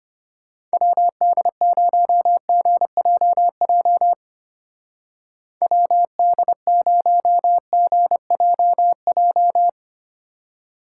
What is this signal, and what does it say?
Morse code, heard as a keyed tone, about 32 wpm, 700 Hz. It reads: WD0GJJ WD0GJJ